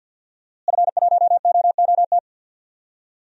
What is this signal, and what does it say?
Morse code, heard as a keyed tone, about 51 wpm, 700 Hz. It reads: F1OOT
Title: DXCC-100 Call Signs - 5 Characters - Repeated 2x - 50wpm